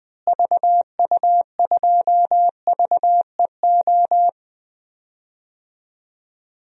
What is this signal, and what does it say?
Morse code, heard as a keyed tone, about 20 wpm, 700 Hz. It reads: VU2VEO
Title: DXCC-100 Call Signs - 6 Characters - Repeated 2x - 20wpm